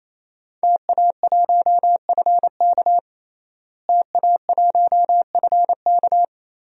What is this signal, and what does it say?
Morse code, heard as a keyed tone, about 28 wpm, 700 Hz. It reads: TA1FK TA1FK
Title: DXCC-100 Call Signs - 5 Characters - Repeated 3x - 28wpm